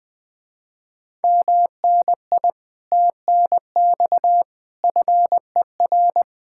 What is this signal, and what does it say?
Morse code, heard as a keyed tone, about 24 wpm, 700 Hz. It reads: MNI TNX FER
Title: Speed-Racing Common QSO Phrases - 20wpm